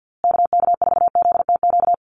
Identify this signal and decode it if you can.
Morse code, heard as a keyed tone, about 48 wpm, 700 Hz. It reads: KK4ZTQ
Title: USA Call Signs Repeated 3x 50wpm